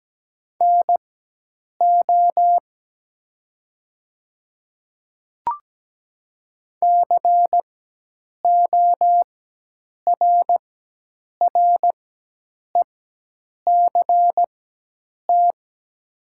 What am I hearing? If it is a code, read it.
Morse code, heard as a keyed tone, about 17 wpm, spaced as for 8 wpm, 700 Hz. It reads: NO CORRECT
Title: 4x Character Spacing - Top 500 Words - 17wpm